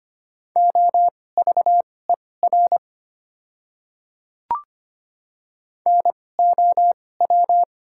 Morse code, heard as a keyed tone, about 25 wpm, spaced as for 18 wpm, 690 Hz. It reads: OVER NOW
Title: 2x Character Spacing - Top 100 Words - 25wpm